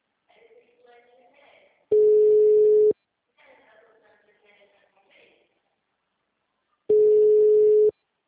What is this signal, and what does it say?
Ringback tone (Europe, 425 Hz), with speech.
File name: out- -2004- - - .83914